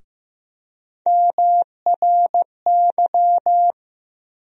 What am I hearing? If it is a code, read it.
Morse code, heard as a keyed tone, about 15 wpm, 700 Hz. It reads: MRY
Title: USA Call Sign Postfixes - 3 Letters 15wpm